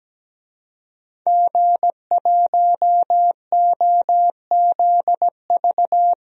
Morse code, heard as a keyed tone, about 17 wpm, 700 Hz. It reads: G1OZV
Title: DXCC-100 Call Signs - 5 Characters - Repeated 2x - 17wpm